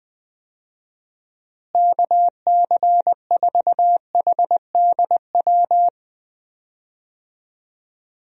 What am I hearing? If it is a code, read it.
Morse code, heard as a keyed tone, about 20 wpm, 700 Hz. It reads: KC4HDW
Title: USA Call Signs - 6 Characters - Repeated 2x - 20wpm